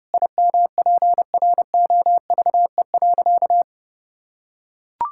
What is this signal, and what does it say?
Morse code, heard as a keyed tone, about 30 wpm, 700 Hz. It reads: IMPROVE.
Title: Sentences from Top 1000 Words - Review Part 5 - 30wpm